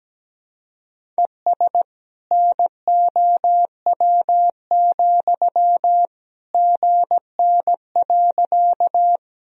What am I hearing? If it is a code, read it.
Morse code, heard as a keyed tone, about 17 wpm, 700 Hz. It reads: ES NOW, GN.
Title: Sentences from Top 300 Words and Abbreviations 17wpm